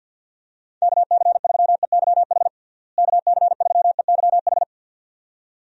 Morse code, heard as a keyed tone, about 47 wpm, 700 Hz. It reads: KC3EYH KC3EYH
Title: USA Call Signs - 6 Characters - Repeated 2x - 50wpm